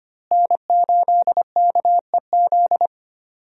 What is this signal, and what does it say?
Morse code, heard as a keyed tone, about 25 wpm, 700 Hz. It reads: N8KEZ
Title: USA Call Signs - 5 Characters - Repeated 2x - 25wpm